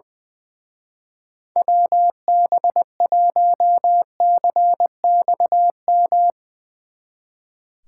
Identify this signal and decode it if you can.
Morse code, heard as a keyed tone, about 22 wpm, 700 Hz. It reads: WB1CXM
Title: Speed Racing - USA 2x3 Call Signs 20wpm